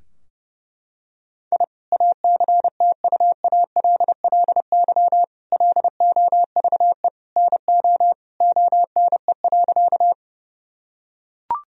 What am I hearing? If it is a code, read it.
Morse code, heard as a keyed tone, about 30 wpm, 700 Hz. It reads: I ACTUALLY LOVE NO ONE.
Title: Sentences from Top 1000 Words - Review Part 4 - 30wpm